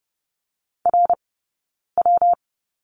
Morse code, heard as a keyed tone, about 30 wpm, 700 Hz. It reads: R W
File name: Lesson 19d - Letter w - Two Characters 30wpm